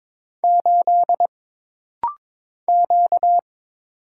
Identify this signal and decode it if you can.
Morse code, heard as a keyed tone, about 22 wpm, 700 Hz. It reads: 8 Q
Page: This rapid-fire timing is geared to push your ICR (Instant Character Recognition) skills.